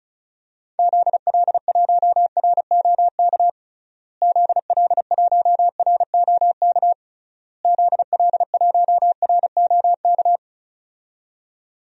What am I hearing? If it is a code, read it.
Morse code, heard as a keyed tone, about 36 wpm, 700 Hz. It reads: ZL1ROK ZL1ROK ZL1ROK
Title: DXCC-100 Call Signs - 6 Characters - Repeated 3x - 35wpm